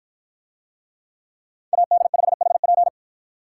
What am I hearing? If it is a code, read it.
Morse code, heard as a keyed tone, about 53 wpm, 700 Hz. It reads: AD5HP